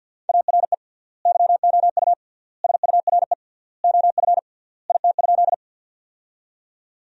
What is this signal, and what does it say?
Morse code, heard as a keyed tone, about 53 wpm, 700 Hz. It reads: ARE YOU SURE OF IT?
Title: 3x Word-Spacing - Sentences from Top 300 Words - 50wpm